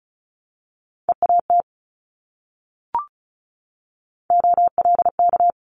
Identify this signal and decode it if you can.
Morse code, heard as a keyed tone, about 36 wpm, 700 Hz. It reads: EAT OLK
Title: Multi-letter Phonemes 35wpm